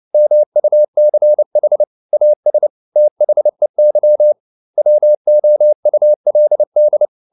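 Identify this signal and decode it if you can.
Morse code, heard as a keyed tone, about 29 wpm, 600 Hz. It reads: MUCH AS THEY WOULD
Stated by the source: Morse code at 29 wpm, sending Ken Follett's novel Never in English, from CW med Gnister.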